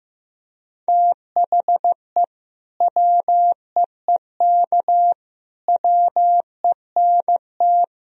Morse code, heard as a keyed tone, about 15 wpm, 700 Hz. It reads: THE WEEK WENT
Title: Sentences from Top 1000 Words - Review Part 2 - 15wpm